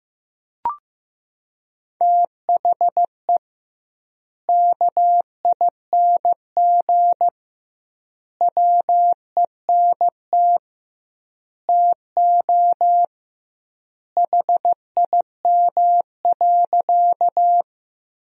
Morse code, heard as a keyed tone, about 15 wpm, 700 Hz. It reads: THE KING WENT TO HIM.
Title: 2x Word-Spacing - Sentences from Top 400 Words - 15wpm